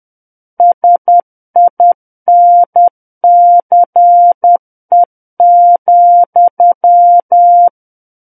Morse code, heard as a keyed tone, about 10 wpm, 700 Hz. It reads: SINCE,
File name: Book One / 01-Chapter 1 - 10 WPM